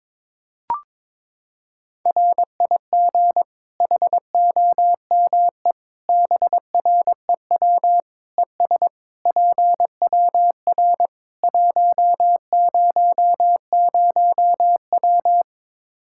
Morse code, heard as a keyed tone, about 22 wpm, 700 Hz. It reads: RIG HOME BREW ES PWR 100W
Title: Practice QSO - Exchange 2 - Info - 22wpm